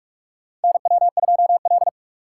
Morse code, heard as a keyed tone, about 46 wpm, 700 Hz. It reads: NW2L
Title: USA Call Signs - 3x Character Spacing - 45wpm